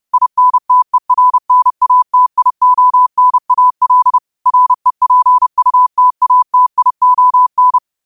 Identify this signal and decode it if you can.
Morse code, heard as a keyed tone, about 30 wpm, 1000 Hz. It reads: INTERNATIONAL REPUTATION